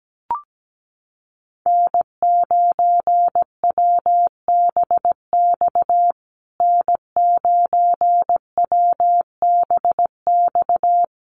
Morse code, heard as a keyed tone, about 17 wpm, 700 Hz. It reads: N9WBX N9WBX
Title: USA Call Signs Repeated 2x 17wpm